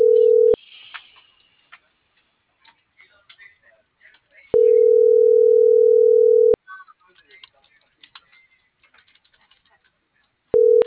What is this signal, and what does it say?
Ringback tone (North America), 440 + 480 Hz.